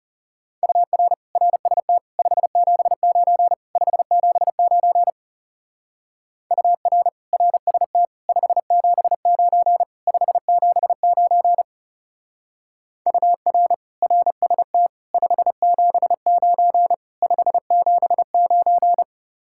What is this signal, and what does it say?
Morse code, heard as a keyed tone, about 35 wpm, 700 Hz. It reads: UR RST 579 579 UR RST 579 579 UR RST 579 579
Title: Speed-Racing Common QSO Phrases - 30wpm